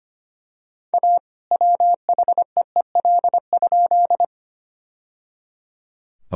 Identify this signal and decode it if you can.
Morse code, heard as a keyed tone, about 25 wpm, 700 Hz. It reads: A WHEEL?
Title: Sentences from Top 1000 Words - Review Part 3 - 25wpm